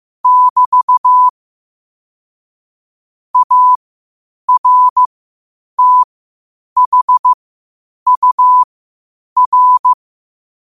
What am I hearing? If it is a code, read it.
Morse code, heard as a keyed tone, about 15 wpm, spaced as for 8 wpm, 1000 Hz. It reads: <BT> ARTHUR
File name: qotdfiles / 8wpm